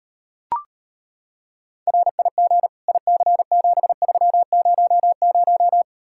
Morse code, heard as a keyed tone, about 29 wpm, 700 Hz. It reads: RIG IC7300
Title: Speed-Racing Common QSO Phrases Encore - 25wpm